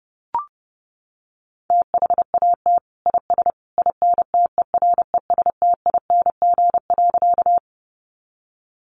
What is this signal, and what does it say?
Morse code, heard as a keyed tone, about 30 wpm, 700 Hz. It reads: THAT IS INTERESTING.
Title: Sentences from Top 1000 Words 30wpm